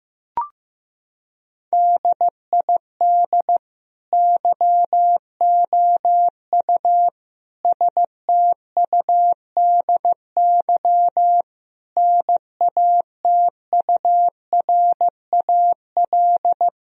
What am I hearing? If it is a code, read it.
Morse code, heard as a keyed tone, about 16 wpm, 700 Hz. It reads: DID YOU STUDY NATURAL
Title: Sentences from Top 1000 Words - Review Part 4 - 15wpm